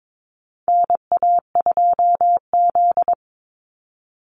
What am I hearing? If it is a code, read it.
Morse code, heard as a keyed tone, about 22 wpm, 700 Hz. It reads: NA2Z
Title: USA Call Signs 22wpm